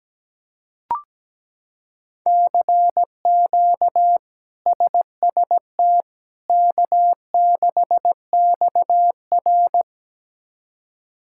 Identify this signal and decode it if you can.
Morse code, heard as a keyed tone, about 17 wpm, 700 Hz. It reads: CQ SST K6XR
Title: K1USN SST QSO Practice - 17wpm